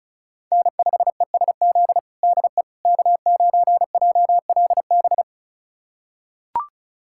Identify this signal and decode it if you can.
Morse code, heard as a keyed tone, about 36 wpm, 700 Hz. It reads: N5ESZ DE K9JLB
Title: USA Callsign DE Callsign - 5 Characters - 35wpm